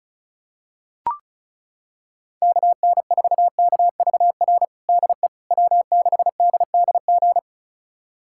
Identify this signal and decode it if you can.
Morse code, heard as a keyed tone, about 34 wpm, 700 Hz. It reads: KN4KVR DE W6DDG
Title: USA Callsign DE Callsign 35wpm